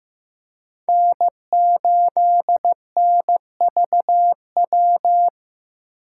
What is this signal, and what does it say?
Morse code, heard as a keyed tone, about 18 wpm, 700 Hz. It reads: N8NVW
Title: Speed Racing - USA 1x3 Call Signs 15wpm